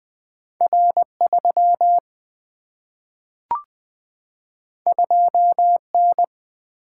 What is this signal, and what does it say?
Morse code, heard as a keyed tone, about 20 wpm, 700 Hz. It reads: R 3 2 N
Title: Single-Letter Single-Number 20wpm